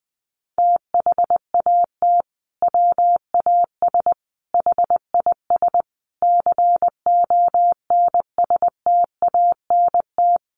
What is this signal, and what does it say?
Morse code, heard as a keyed tone, about 20 wpm, 700 Hz. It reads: THAT WAS HIS CONSTANT
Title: Sentences from Top 1500 Words 20wpm